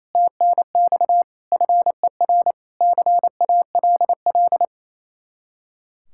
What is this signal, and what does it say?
Morse code, heard as a keyed tone, about 34 wpm, 700 Hz. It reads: TNX FER CALL
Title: Speed-Racing Common QSO Phrases Encore - 28wpm